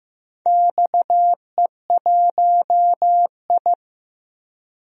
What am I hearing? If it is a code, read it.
Morse code, heard as a keyed tone, about 15 wpm, 700 Hz. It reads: XE1I
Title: DXCC-100 Call Signs - 4 Characters - Repeated 3x - 15wpm